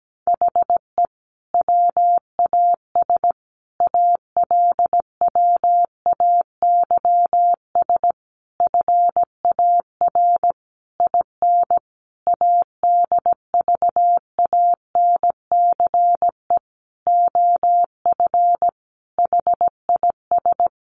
Morse code, heard as a keyed tone, about 17 wpm, 700 Hz. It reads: HE WAS ALWAYS FAR IN ADVANCE OF HIS
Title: Sentences from Top 1400 Words 17wpm